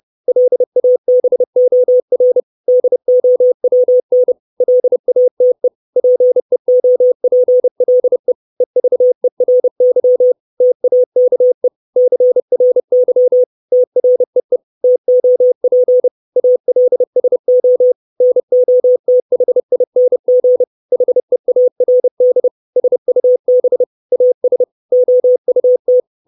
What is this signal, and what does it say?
Morse code, heard as a keyed tone, about 30 wpm, 500 Hz. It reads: LABOR DOWN LATE PEOPLE EVERY TAKE CRY TREE TOP ALSO NOTHING HEARD SUB AS OUT
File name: 632 Words 30WPM